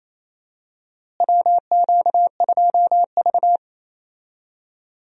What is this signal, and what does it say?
Morse code, heard as a keyed tone, about 28 wpm, 700 Hz. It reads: WQ2V
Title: USA Call Signs - 4 Characters - Repeated 2x - 28wpm